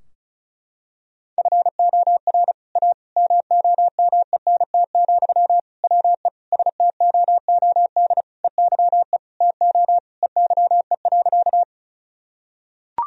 Morse code, heard as a keyed tone, about 34 wpm, 700 Hz. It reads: FOR A MOMENT, WE STOOD EYE TO EYE.
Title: Sentences from Top 1000 Words - Review Part 2 - 35wpm